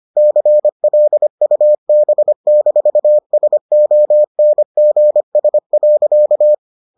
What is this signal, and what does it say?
Morse code, heard as a keyed tone, about 25 wpm, 600 Hz. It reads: CLUB-SONGS.